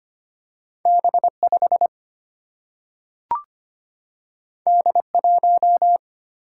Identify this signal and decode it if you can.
Morse code, heard as a keyed tone, about 25 wpm, 700 Hz. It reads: B 5 D 1
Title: Single-Letter Single-Number 25wpm